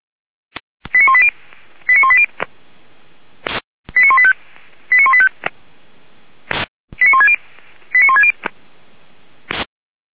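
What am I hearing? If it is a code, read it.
Five-tone selective call (ZVEI), heard as quick runs of five tones, tones of about 70 ms. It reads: 89180 89180 89185 89185 89160 89160
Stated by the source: ZVEI five-tone call received by SDR, from the feuerwehr-orlamuende.org fire brigade audio archive.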